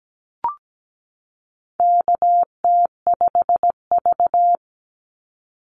Morse code, heard as a keyed tone, about 17 wpm, 700 Hz. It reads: KT5V
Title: Call Signs Repeated 2x - 17wpm